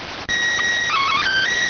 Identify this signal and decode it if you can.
SSTV (slow-scan television), PD 290.